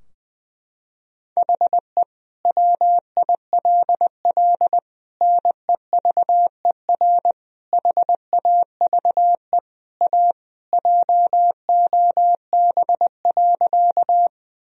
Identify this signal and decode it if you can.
Morse code, heard as a keyed tone, about 20 wpm, 700 Hz. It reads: HE WILL NEVER HAVE A JOB.
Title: Sentences from Top 1000 Words - Review Part 3 - 20wpm